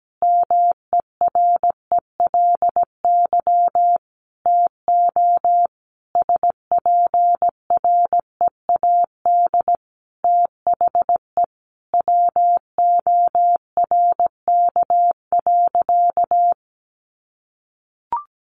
Morse code, heard as a keyed tone, about 17 wpm, 700 Hz. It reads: MERELY TO SPREAD THE WORK.
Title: Sentences from Top 1400 Words 17wpm